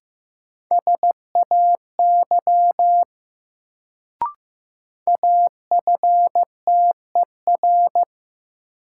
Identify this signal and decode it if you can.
Morse code, heard as a keyed tone, about 15 wpm, 700 Hz. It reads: SAY AFTER
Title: Sentences from Top 200 Words - New Word Drill - 15wpm